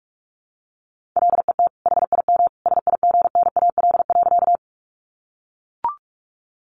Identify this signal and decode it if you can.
Morse code, heard as a keyed tone, about 44 wpm, 700 Hz. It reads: LET HIM SIGNAL.
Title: Sentences from Top 1400 Words 45wpm